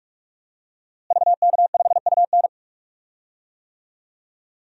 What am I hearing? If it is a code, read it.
Morse code, heard as a keyed tone, about 43 wpm, 700 Hz. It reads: VK5UN
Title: DXCC-100 Call Signs - 5 Characters - Repeated 3x - 45wpm